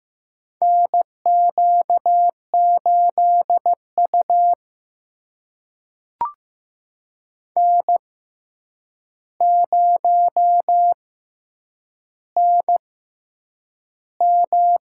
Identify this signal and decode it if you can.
Morse code, heard as a keyed tone, about 15 wpm, spaced as for 8 wpm, 700 Hz. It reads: NQ8U N0NM
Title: USA Call Signs - 6x Character Spacing - 15wpm